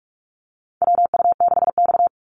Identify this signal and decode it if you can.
Morse code, heard as a keyed tone, about 44 wpm, 700 Hz. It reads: WU6X
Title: Call Signs Repeated 2x - 45wpm